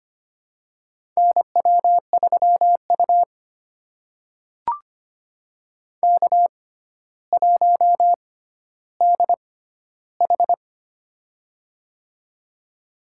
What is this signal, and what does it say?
Morse code, heard as a keyed tone, about 25 wpm, spaced as for 13 wpm, 700 Hz. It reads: NW3U K1DH
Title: USA Call Signs - 4 Characters - 6x Character Spacing - 25wpm